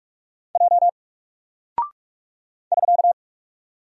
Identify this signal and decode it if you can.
Morse code, heard as a keyed tone, about 44 wpm, 700 Hz. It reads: J <SK>
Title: Single Character (Letter:Number:Punctuation:Prosign) 45wpm